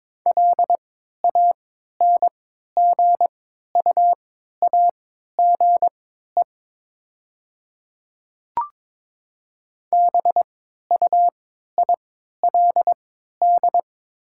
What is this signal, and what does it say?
Morse code, heard as a keyed tone, about 22 wpm, spaced as for 12 wpm, 690 Hz. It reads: LANGUAGE BUILD
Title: 3x Character Spacing - Top 500 Words - 22wpm